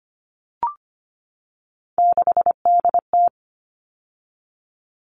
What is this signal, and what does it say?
Morse code, heard as a keyed tone, about 25 wpm, 700 Hz. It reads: 6DT
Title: All Cut Numbers - 3-Digit 25wpm